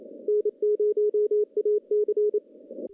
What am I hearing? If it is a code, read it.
Morse code, heard as a keyed tone, about 30 wpm, 420 Hz. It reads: N0AC
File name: contests / CW-SS-k5tr- -0100